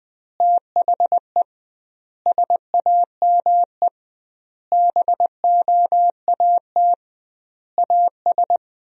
Morse code, heard as a keyed tone, about 20 wpm, 700 Hz. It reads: THE SAME BOAT AS
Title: 2x Word-Spacing - Sentences from Top 500 Words - 20wpm